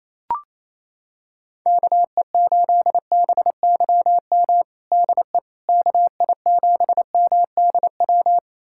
Morse code, heard as a keyed tone, about 28 wpm, 700 Hz. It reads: KE8BYM DE KI7MDW
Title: USA Callsign DE Callsign - 6 Characters - 28wpm